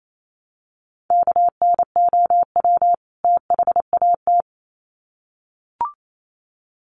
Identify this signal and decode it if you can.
Morse code, heard as a keyed tone, about 28 wpm, 700 Hz. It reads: KNOW THAT
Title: Sets of 2 Words 28wpm